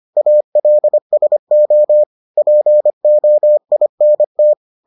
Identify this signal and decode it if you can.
Morse code, heard as a keyed tone, about 25 wpm, 600 Hz. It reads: ALSO POINT